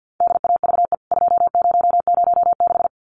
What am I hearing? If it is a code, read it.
Morse code, heard as a keyed tone, about 50 wpm, 700 Hz. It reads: DAVE 2006
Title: CWT Exchange 50wpm